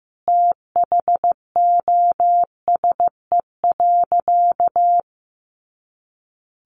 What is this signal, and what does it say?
Morse code, heard as a keyed tone, about 15 wpm, 700 Hz. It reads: THOSE.
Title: A Sound Of Thunder 15wpm